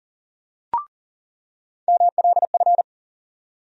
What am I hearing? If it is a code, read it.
Morse code, heard as a keyed tone, about 37 wpm, 700 Hz. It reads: MLF
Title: USA Call Sign Postfixes - 3 Letters 40wpm